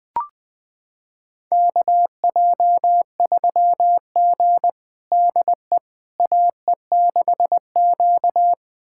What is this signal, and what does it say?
Morse code, heard as a keyed tone, about 20 wpm, 700 Hz. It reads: KJ3G DE AE6Q
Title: USA Callsign DE Callsign - 4 Characters - 20wpm